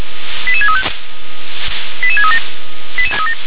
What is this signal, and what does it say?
Five-tone selective call (ZVEI), heard as quick runs of five tones, tones of about 70 ms. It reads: 88635 88637